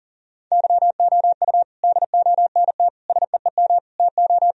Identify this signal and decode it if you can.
Morse code, heard as a keyed tone, about 40 wpm, 700 Hz. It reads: YOU DONT SEEM TO
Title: Sentences from Top 1000 Words - Review Part 1 - 40wpm